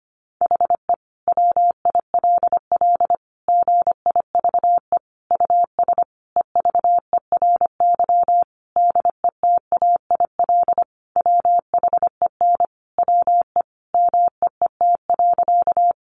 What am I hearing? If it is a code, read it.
Morse code, heard as a keyed tone, about 25 wpm, 700 Hz. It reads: HE WILL GIVE US EVERY DETAIL WHEN WE MEET.
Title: Sentences from Top 1500 Words 25wpm